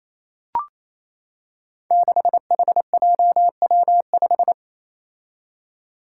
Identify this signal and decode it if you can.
Morse code, heard as a keyed tone, about 27 wpm, 700 Hz. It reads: 6HJW5
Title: Five Challenging Letter-Number Combinations 28wpm